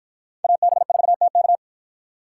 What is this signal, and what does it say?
Morse code, heard as a keyed tone, about 53 wpm, 700 Hz. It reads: AB4TK